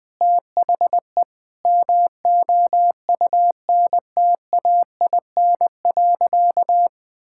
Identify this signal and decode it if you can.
Morse code, heard as a keyed tone, about 21 wpm, 700 Hz. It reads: THE MOUNTAIN.
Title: Sentences from Top 1000 Words - Review Part 6 - 20wpm